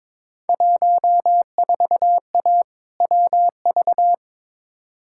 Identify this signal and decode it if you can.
Morse code, heard as a keyed tone, about 22 wpm, 700 Hz. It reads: 14A WV
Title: Field Day Exchanges Encore - 22wpm